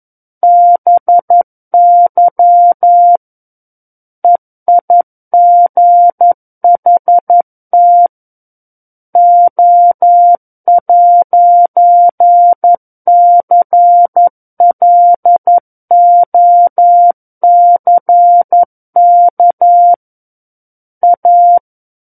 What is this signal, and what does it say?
Morse code, heard as a keyed tone, about 11 wpm, 700 Hz. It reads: BY EIGHT O'CLOCK A
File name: 11 WPM